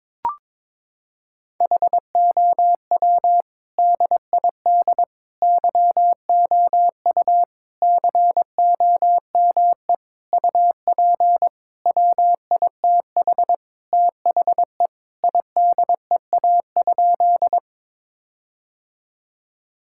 Morse code, heard as a keyed tone, about 22 wpm, 700 Hz. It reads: HOW DID YOU COME UP WITH THE IDEA?
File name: practice / 22wpm / Sentences with 7-Word Phrases - 22wpm